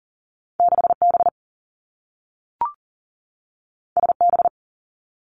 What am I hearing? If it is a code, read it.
Morse code, heard as a keyed tone, about 38 wpm, 700 Hz. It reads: 6B SB
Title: Two Challenging Letter-Number Combinations 40wpm